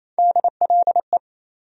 Morse code, heard as a keyed tone, about 28 wpm, 700 Hz. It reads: DLE
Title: USA Call Sign Postfixes - 3 Letters 28wpm